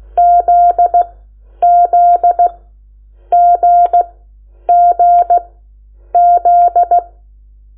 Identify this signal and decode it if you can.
Morse code, heard as a keyed tone, about 16 wpm, spaced as for 9 wpm, 700 Hz. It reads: ZZGGZ
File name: Lecciones / CLASE Nº8